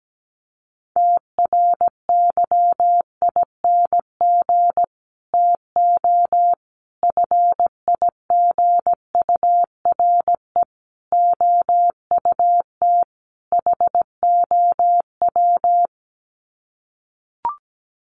Morse code, heard as a keyed tone, about 17 wpm, 700 Hz. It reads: TRYING TO FIGURE OUT HOW
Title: Sets of 5 Words Encore 17wpm